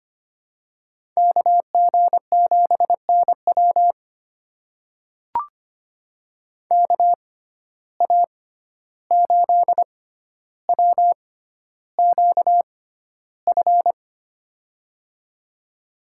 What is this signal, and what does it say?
Morse code, heard as a keyed tone, about 25 wpm, spaced as for 12 wpm, 700 Hz. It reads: KG7NW KA8WQF
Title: USA Call Signs - 6x Character Spacing - 25wpm